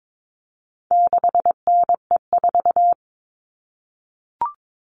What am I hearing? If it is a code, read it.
Morse code, heard as a keyed tone, about 22 wpm, 700 Hz. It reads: 6NE4
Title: All Cut Numbers - 4-Digit 22wpm